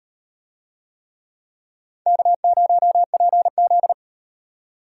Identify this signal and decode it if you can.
Morse code, heard as a keyed tone, about 38 wpm, 700 Hz. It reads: K0PZ